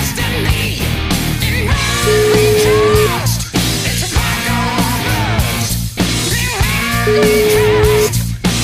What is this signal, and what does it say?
Ringback tone (Europe, 425 Hz), with noise.